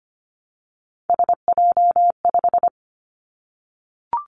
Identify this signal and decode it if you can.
Morse code, heard as a keyed tone, about 25 wpm, 700 Hz. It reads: SJ5
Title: Three Challenging Letter-Number Combinations 25wpm